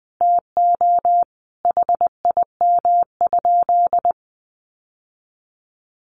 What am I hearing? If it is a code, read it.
Morse code, heard as a keyed tone, about 20 wpm, 700 Hz. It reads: TO HIM?
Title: Sentences from Top 1100 Words 20wpm